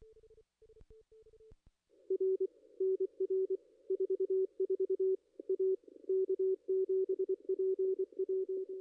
Morse code, heard as a keyed tone, about 24 wpm, 380 Hz, with radio noise. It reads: R NR 44 A K7PW
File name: contests / ss-cw-k5tr-op-n5rz -2300